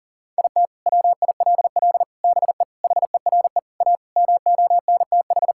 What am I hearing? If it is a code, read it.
Morse code, heard as a keyed tone, about 37 wpm, 700 Hz. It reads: IT WILL BE HERE A MONTH
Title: Sentences from Top 1000 Words - Review Part 5 - 40wpm